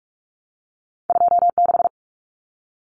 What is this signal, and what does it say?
Morse code, heard as a keyed tone, about 46 wpm, 700 Hz. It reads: 26
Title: Common Cut Numbers - 2-Digit 45wpm